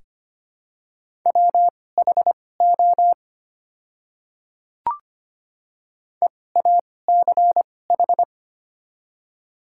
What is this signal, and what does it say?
Morse code, heard as a keyed tone, about 25 wpm, spaced as for 18 wpm, 700 Hz. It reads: WHO EACH
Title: 2x Character Spacing - Top 100 Words - 25wpm